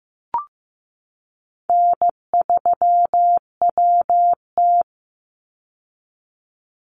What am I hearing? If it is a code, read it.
Morse code, heard as a keyed tone, about 15 wpm, 700 Hz. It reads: N3WT
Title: Call Signs Repeated 3x - 15wpm